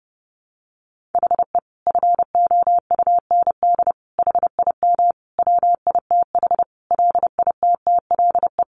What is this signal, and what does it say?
Morse code, heard as a keyed tone, about 30 wpm, 700 Hz. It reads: HE FOUND HIM WITH LITTLE
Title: Sentences from Top 1500 Words 30wpm